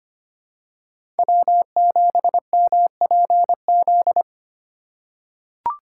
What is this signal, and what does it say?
Morse code, heard as a keyed tone, about 25 wpm, 700 Hz. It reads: W7MPZ
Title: USA Call Signs - 5 Characters - 4x Character Spacing - 25wpm